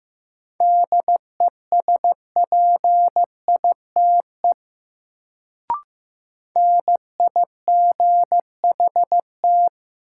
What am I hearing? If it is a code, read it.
Morse code, heard as a keyed tone, about 15 wpm, 700 Hz. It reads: DESPITE NIGHT
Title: Sets of 4 Words - New Word Drill - 15wpm